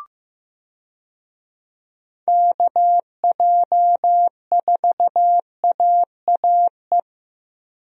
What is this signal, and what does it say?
Morse code, heard as a keyed tone, about 15 wpm, 700 Hz. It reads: KJ4AAE
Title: USA Call Signs - 6 Characters - Repeated 2x - 15wpm